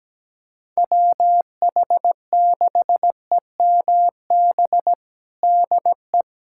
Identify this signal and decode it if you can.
Morse code, heard as a keyed tone, about 17 wpm, 700 Hz. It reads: WH6EMB DE
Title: USA Callsign DE Callsign 17wpm